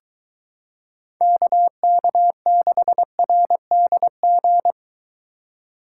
Morse code, heard as a keyed tone, about 23 wpm, 700 Hz. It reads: KK6RDG